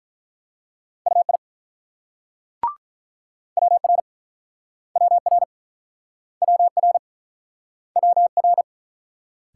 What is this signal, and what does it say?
Morse code, heard as a keyed tone, about 48 wpm, 700 Hz. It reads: UI WR WR WR WR